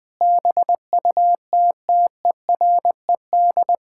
Morse code, heard as a keyed tone, about 20 wpm, 700 Hz. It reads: BUTTERED
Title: Pangram Challenge v2 - 20wpm